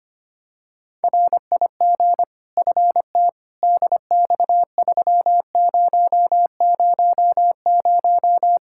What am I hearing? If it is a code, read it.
Morse code, heard as a keyed tone, about 25 wpm, 700 Hz. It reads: RIG FT DX3000
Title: K1USN SST QSO Practice 25wpm